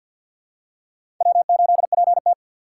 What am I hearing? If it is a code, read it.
Morse code, heard as a keyed tone, about 51 wpm, 700 Hz. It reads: W8PT
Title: USA Call Signs - 4 Characters - 2x Character Spacing - 50wpm